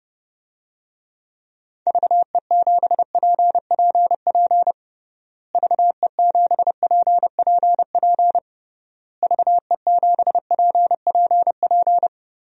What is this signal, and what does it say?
Morse code, heard as a keyed tone, about 32 wpm, 700 Hz. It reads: VE7PPP VE7PPP VE7PPP
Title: DXCC-100 Call Signs - 6 Characters - Repeated 3x - 30wpm